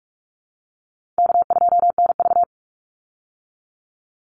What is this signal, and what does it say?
Morse code, heard as a keyed tone, about 44 wpm, 700 Hz. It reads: K2NV
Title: Easy Call Signs Repeated 3x - 45wpm